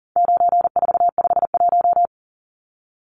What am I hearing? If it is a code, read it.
Morse code, heard as a keyed tone, about 40 wpm, 700 Hz. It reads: 9451
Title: Numbers - 4-Digit 40wpm